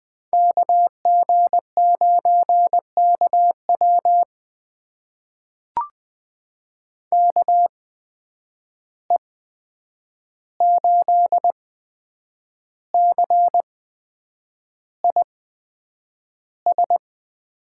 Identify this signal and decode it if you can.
Morse code, heard as a keyed tone, about 20 wpm, spaced as for 8 wpm, 700 Hz. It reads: KG9KW KE8CIS
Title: USA Call Signs - 8x Character Spacing - 20wpm